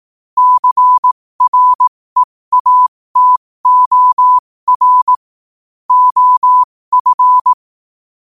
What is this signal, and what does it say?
Morse code, heard as a keyed tone, about 18 wpm, spaced as for 15 wpm, 1000 Hz. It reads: CREATOR OF
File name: qotdfiles / 15wpm